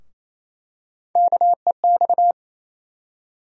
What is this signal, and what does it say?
Morse code, heard as a keyed tone, about 28 wpm, 700 Hz. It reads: KEX
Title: USA Call Sign Postfixes - 3 Letters 28wpm